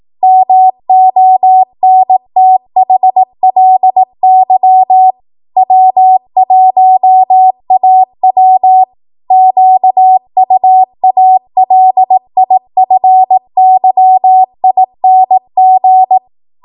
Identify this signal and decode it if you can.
Morse code, heard as a keyed tone, about 18 wpm, 750 Hz. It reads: MONTHLY W1AW QUALIFYING
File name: Archive / 18 WPM / 18WPM